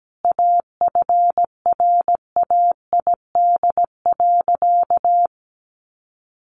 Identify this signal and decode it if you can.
Morse code, heard as a keyed tone, about 17 wpm, 700 Hz. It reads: AFRAID.
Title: Sentences from Top 1500 Words 17wpm